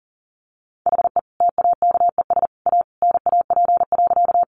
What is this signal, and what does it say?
Morse code, heard as a keyed tone, about 40 wpm, 700 Hz. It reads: HE TAKES A NAP.
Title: Easy Phrases 40wpm